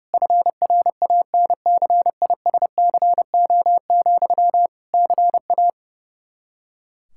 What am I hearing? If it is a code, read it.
Morse code, heard as a keyed tone, about 30 wpm, 700 Hz. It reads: FRANCISCO, CA
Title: Practice QSO - Exchange 1 - Beginning - 30wpm